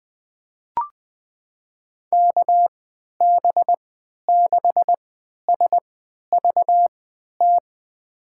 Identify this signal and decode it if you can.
Morse code, heard as a keyed tone, about 20 wpm, spaced as for 11 wpm, 700 Hz. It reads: KB6SVT
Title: USA Call Signs - 6 Characters - 3x Character Spacing - 20wpm